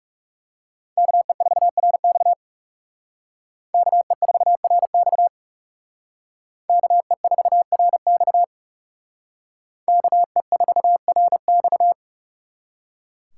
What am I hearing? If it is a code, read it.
Morse code, heard as a keyed tone, about 40 wpm, 700 Hz. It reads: KE4RX KE4RX KE4RX KE4RX